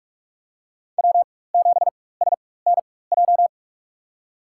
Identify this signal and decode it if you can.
Morse code, heard as a keyed tone, about 44 wpm, spaced as for 21 wpm, 700 Hz. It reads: W7SNJ
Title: USA Call Signs - 5 Characters - 4x Character Spacing - 45wpm